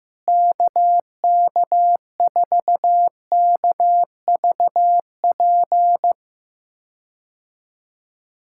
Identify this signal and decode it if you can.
Morse code, heard as a keyed tone, about 15 wpm, 700 Hz. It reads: KK4KVP
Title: K1USN SST QSO Practice - 15wpm